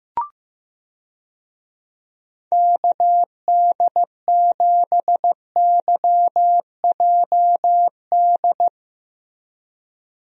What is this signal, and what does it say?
Morse code, heard as a keyed tone, about 15 wpm, 700 Hz. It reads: KD7YJD
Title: USA Call Signs - 6 Characters - Repeated 2x - 15wpm